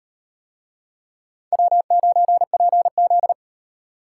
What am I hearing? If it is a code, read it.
Morse code, heard as a keyed tone, about 34 wpm, 700 Hz. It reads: W9PZ